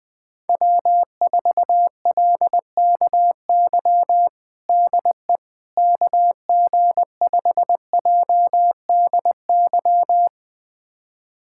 Morse code, heard as a keyed tone, about 20 wpm, 700 Hz. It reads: W4LKY DE KG5JDY
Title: USA Callsign DE Callsign 20wpm